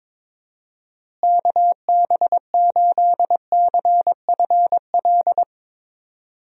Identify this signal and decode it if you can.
Morse code, heard as a keyed tone, about 22 wpm, 700 Hz. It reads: KB8CFL
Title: USA Call Signs - 6 Characters - Repeated 2x - 22wpm